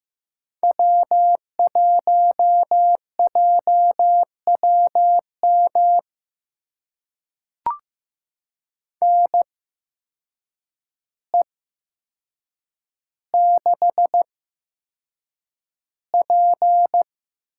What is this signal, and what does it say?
Morse code, heard as a keyed tone, about 15 wpm, spaced as for 7 wpm, 700 Hz. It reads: W1JWM NE6P
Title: USA Call Signs - 8x Character Spacing - 15wpm